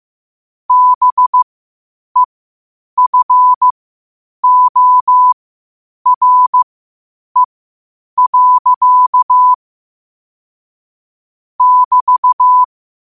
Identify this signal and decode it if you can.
Morse code, heard as a keyed tone, about 15 wpm, spaced as for 8 wpm, 1000 Hz. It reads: BEFORE. <BT>
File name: qotdfiles / 8wpm